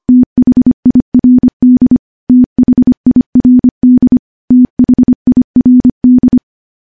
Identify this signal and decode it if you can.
Morse code, heard as a keyed tone, about 25 wpm, 250 Hz. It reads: THIRD THIRD THIRD
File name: morse-training-25WPM